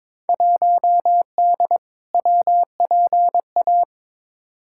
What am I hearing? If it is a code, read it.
Morse code, heard as a keyed tone, about 22 wpm, 700 Hz. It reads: 1D WPA
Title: Field Day Exchanges - 22wpm